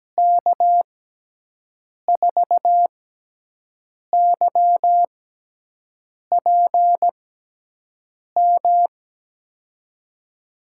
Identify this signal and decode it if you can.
Morse code, heard as a keyed tone, about 17 wpm, spaced as for 6 wpm, 700 Hz. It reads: K4YPM
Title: USA Call Signs - 6x Character Spacing - 17wpm